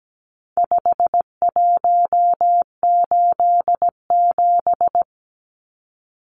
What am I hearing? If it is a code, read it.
Morse code, heard as a keyed tone, about 17 wpm, 700 Hz. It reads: 5187
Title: Numbers - 4-Digit 17wpm